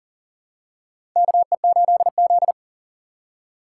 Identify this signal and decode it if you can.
Morse code, heard as a keyed tone, about 40 wpm, 700 Hz. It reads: KE8Z
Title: USA Call Signs - 4 Characters - 2x Character Spacing - 40wpm